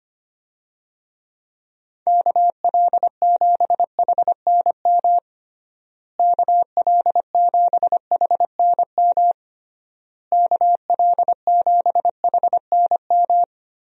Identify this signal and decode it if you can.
Morse code, heard as a keyed tone, about 25 wpm, 700 Hz. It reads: KL7HNM KL7HNM KL7HNM
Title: USA Call Signs - 6 Characters - Repeated 3x - 25wpm